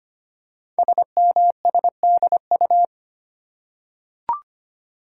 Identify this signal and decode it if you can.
Morse code, heard as a keyed tone, about 25 wpm, 700 Hz. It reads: SMSDU
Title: Code Groups - 5 Letters - 25wpm